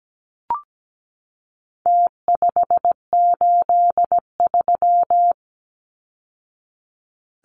Morse code, heard as a keyed tone, about 17 wpm, 700 Hz. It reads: T583
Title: Numbers - Contest Style - 4-Digit 17wpm